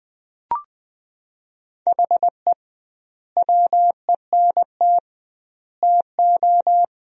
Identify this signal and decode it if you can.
Morse code, heard as a keyed tone, about 21 wpm, 700 Hz. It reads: HE WENT TO
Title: 2x Word-Spacing - Sentences from Top 200 Words - 20wpm